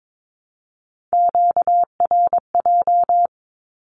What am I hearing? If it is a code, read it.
Morse code, heard as a keyed tone, about 22 wpm, 700 Hz. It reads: QRJ
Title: Q-Codes Repeated 3x 22wpm